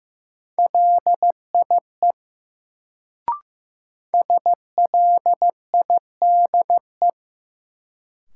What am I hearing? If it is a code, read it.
Morse code, heard as a keyed tone, about 15 wpm, 700 Hz. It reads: LIE SLIDE
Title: Sentences from Top 1600 Words - New Word Drill Alt - 15wpm